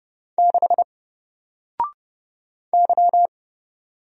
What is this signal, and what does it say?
Morse code, heard as a keyed tone, about 30 wpm, 700 Hz. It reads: 6 Y
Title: Single Letter-Number 30wpm